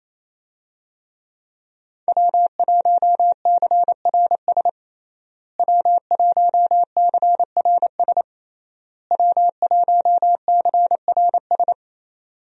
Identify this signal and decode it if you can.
Morse code, heard as a keyed tone, about 28 wpm, 700 Hz. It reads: W1CRS W1CRS W1CRS
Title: USA Call Signs - 5 Characters - Repeated 3x - 28wpm